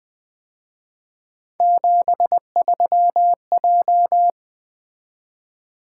Morse code, heard as a keyed tone, about 23 wpm, 700 Hz. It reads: 73J
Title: Speed-Racing - Three Letter-Number Combinations - 20wpm